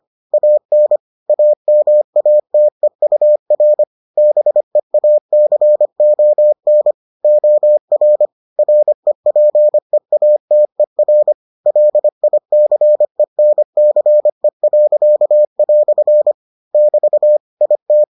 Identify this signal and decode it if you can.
Morse code, heard as a keyed tone, about 25 wpm, 600 Hz. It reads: AN AMATEUR BEACON OR REPEATER LICENCE." <BT> IT